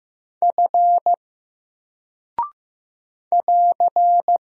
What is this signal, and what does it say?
Morse code, heard as a keyed tone, about 15 wpm, 700 Hz. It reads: F <AR>
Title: Single Character (Letter:Number:Punctuation:Prosign) 15wpm